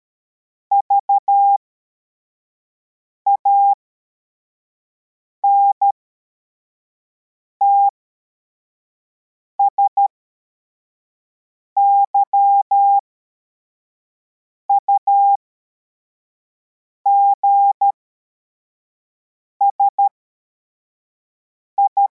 Morse code, heard as a keyed tone, about 13 wpm, spaced as for 4 wpm, 800 Hz. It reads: VANTS YUGSI